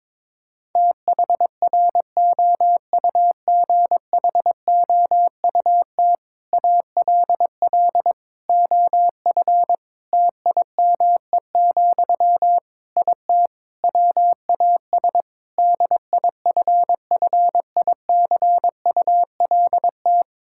Morse code, heard as a keyed tone, about 22 wpm, 700 Hz. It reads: THROUGHOUT ALL OF TIME, IT WAS DIFFICULT
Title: Sentences from Top 1000 Words - Review Part 5 - 22wpm